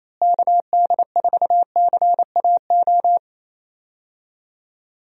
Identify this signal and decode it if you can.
Morse code, heard as a keyed tone, about 28 wpm, 700 Hz. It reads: KD4CAO
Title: USA Call Signs - 6 Characters - Repeated 2x - 28wpm